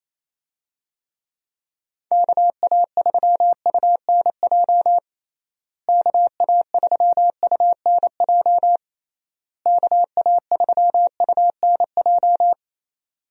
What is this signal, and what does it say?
Morse code, heard as a keyed tone, about 28 wpm, 700 Hz. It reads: KA3UNJ KA3UNJ KA3UNJ
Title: USA Call Signs - 6 Characters - Repeated 3x - 28wpm